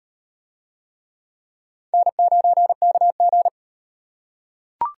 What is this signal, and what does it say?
Morse code, heard as a keyed tone, about 35 wpm, 700 Hz. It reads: N9KG